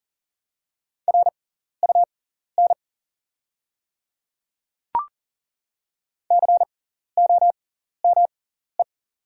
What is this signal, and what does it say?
Morse code, heard as a keyed tone, about 37 wpm, spaced as for 14 wpm, 700 Hz. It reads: RUN COME
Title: 6x Character Spacing - Top 300 Words - 40wpm